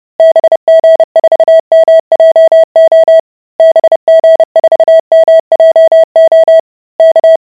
Morse code, heard as a keyed tone, about 30 wpm, 650 Hz. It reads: BG4MJO BG4MJO K